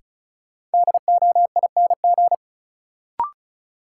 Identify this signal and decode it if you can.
Morse code, heard as a keyed tone, about 36 wpm, 700 Hz. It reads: DOING
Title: Sentences from Top 1400 Words - New Word Drill Alt - 35wpm